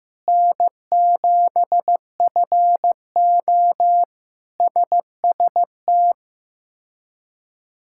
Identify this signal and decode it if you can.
Morse code, heard as a keyed tone, about 16 wpm, 700 Hz. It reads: N7FO SST
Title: K1USN SST QSO Practice - 15wpm